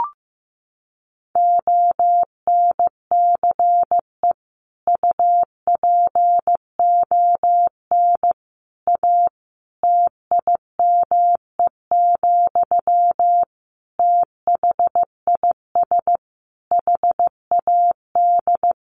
Morse code, heard as a keyed tone, about 15 wpm, 700 Hz. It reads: ONCE UPON A TIME, THIS HAD
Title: Sentences from Top 1500 Words 15wpm